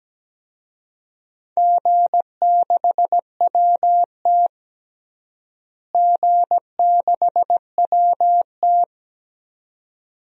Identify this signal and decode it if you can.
Morse code, heard as a keyed tone, about 17 wpm, 700 Hz. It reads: G6WT G6WT
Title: DXCC-100 Call Signs - 4 Characters - Repeated 2x - 17wpm